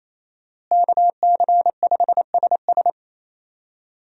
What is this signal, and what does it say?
Morse code, heard as a keyed tone, about 28 wpm, 700 Hz. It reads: KC5SS
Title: USA Call Signs - 5 Characters - Repeated 2x - 28wpm